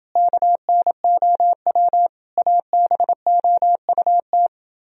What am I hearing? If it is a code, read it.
Morse code, heard as a keyed tone, about 28 wpm, 700 Hz. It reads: KNOW ABOUT